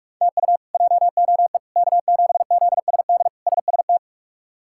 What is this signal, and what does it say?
Morse code, heard as a keyed tone, about 42 wpm, 700 Hz. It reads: TU JOE K7ZSD SST
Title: K1USN SST QSO Practice - 45wpm